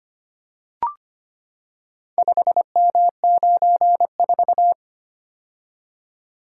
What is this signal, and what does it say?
Morse code, heard as a keyed tone, about 25 wpm, 700 Hz. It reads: 5M94
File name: practice / 25wpm / Four Letter-Number Combinations - 25wpm